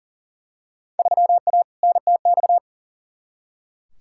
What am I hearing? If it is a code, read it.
Morse code, heard as a keyed tone, about 37 wpm, 700 Hz. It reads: 3A NTX
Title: Field Day Exchanges - 40wpm